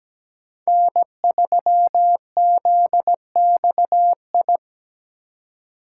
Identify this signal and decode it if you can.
Morse code, heard as a keyed tone, about 17 wpm, 700 Hz. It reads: N3ZXI
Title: USA Call Signs - 5 Characters - 3x Character Spacing - 17wpm